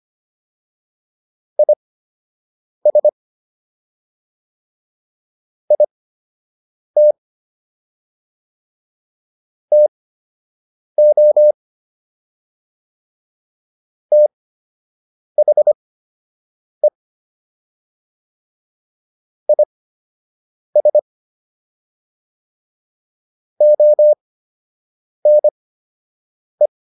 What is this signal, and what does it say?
Morse code, heard as a keyed tone, about 25 wpm, spaced as for 7 wpm, 600 Hz. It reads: IS IT TO THE IS ONE